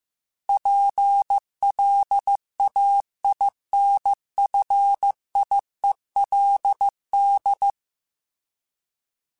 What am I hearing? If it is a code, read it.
Morse code, heard as a keyed tone, about 15 wpm, 800 Hz. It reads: PLAINFIELD
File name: fromSMs / FromSMs mp3s / Cities@13